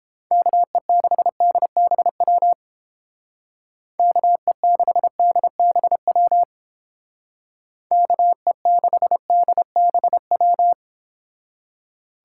Morse code, heard as a keyed tone, about 29 wpm, 700 Hz. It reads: KE6DBW KE6DBW KE6DBW